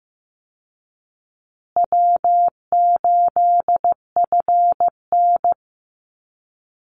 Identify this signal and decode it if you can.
Morse code, heard as a keyed tone, about 15 wpm, 700 Hz. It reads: W8FN
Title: Easy Call Signs Repeated 2x - 15wpm